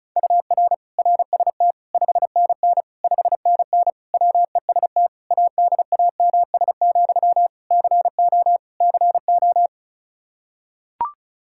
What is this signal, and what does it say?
Morse code, heard as a keyed tone, about 36 wpm, 700 Hz. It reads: UR RST 5NN 5NN WEST ADAMS, CO CO
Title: K1USN SST QSO Practice 35wpm